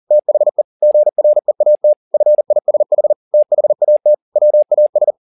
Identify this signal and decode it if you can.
Morse code, heard as a keyed tone, about 40 wpm, 600 Hz. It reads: THE GREAT FISH THAT WAS
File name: the old man and the sea 2 40wpm